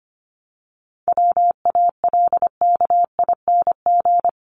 Morse code, heard as a keyed tone, about 25 wpm, 700 Hz. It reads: WALKING
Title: ING Suffix - Top 100 Words with Base Words 25wpm